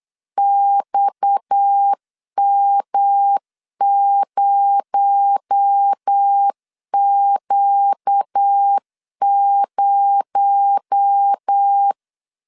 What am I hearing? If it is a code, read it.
Morse code, heard as a keyed tone, about 8 wpm, 800 Hz. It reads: XM0Q0
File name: CW / DET / 1 6 1500C S B 8wpm 5142